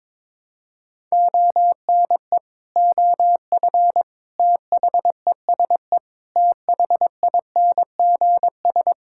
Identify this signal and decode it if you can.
Morse code, heard as a keyed tone, about 22 wpm, 700 Hz. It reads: ONE OF THESE THINGS
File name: practice / 22wpm / Sets of 8 Words - 22wpm